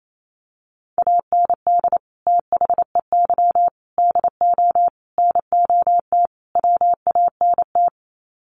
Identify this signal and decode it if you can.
Morse code, heard as a keyed tone, about 28 wpm, 700 Hz. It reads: AND THEY DO NOT WANT
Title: Sets of 5 Words Encore2 28wpm